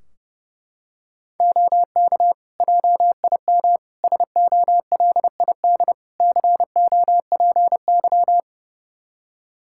Morse code, heard as a keyed tone, about 30 wpm, 700 Hz. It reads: OK JIM SOLID COPY
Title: Practice QSO - Exchange 3 - More Info - 30wpm